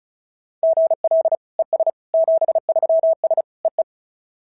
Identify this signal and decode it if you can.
Morse code, heard as a keyed tone, about 36 wpm, 690 Hz. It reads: GL ES 73S EE
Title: SOTA QSO Practice 35wpm